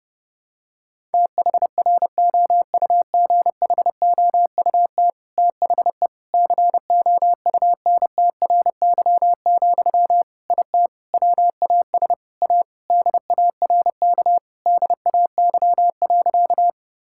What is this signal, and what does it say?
Morse code, heard as a keyed tone, about 30 wpm, 700 Hz. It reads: THROUGHOUT THE COUNTRY, IT WAS A DARK DAY.
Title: Sentences from Top 1000 Words - Review Part 5 - 30wpm